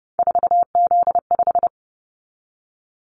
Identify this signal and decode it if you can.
Morse code, heard as a keyed tone, about 30 wpm, 700 Hz. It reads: 4Z5
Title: Callsign Prefixes for Top 50 DXCC - 30wpm